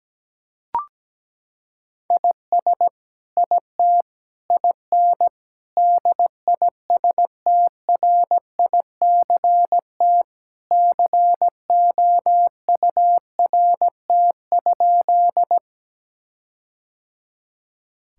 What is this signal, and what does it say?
Morse code, heard as a keyed tone, about 17 wpm, 700 Hz. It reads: IS IT IN DISTRICT COURT?
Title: Sentences from Top 1600 Words 17wpm